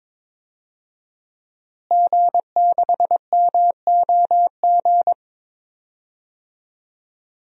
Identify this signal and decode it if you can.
Morse code, heard as a keyed tone, about 22 wpm, 700 Hz. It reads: G6MOG
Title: DXCC-100 Call Signs - 5 Characters - Repeated 3x - 22wpm